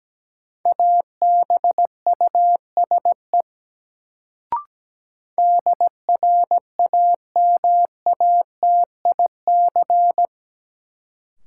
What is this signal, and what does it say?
Morse code, heard as a keyed tone, about 17 wpm, 700 Hz. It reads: ABUSE DRAMATIC
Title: Sentences from Top 1300 Words - New Word Drill - 17wpm